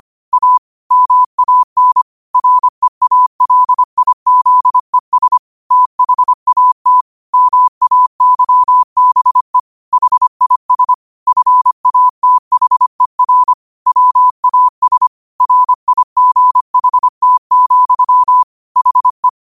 Morse code, heard as a keyed tone, about 25 wpm, 1000 Hz. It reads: A MAN REALIZES THAT MAYBE HIS FATHER WAS RIGHT, HE
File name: qotdfiles / 25wpm